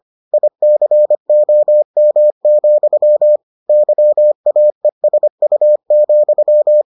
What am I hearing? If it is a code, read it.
Morse code, heard as a keyed tone, about 25 wpm, 600 Hz. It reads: ICOM, YAESU,